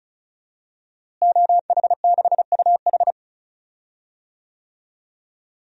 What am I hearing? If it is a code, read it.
Morse code, heard as a keyed tone, about 36 wpm, 700 Hz. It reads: OH6UH
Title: DXCC-100 Call Signs - 5 Characters - Repeated 2x - 35wpm